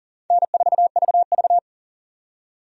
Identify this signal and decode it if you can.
Morse code, heard as a keyed tone, about 40 wpm, 700 Hz. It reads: N4VV
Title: USA Call Signs - 4 Characters - 40wpm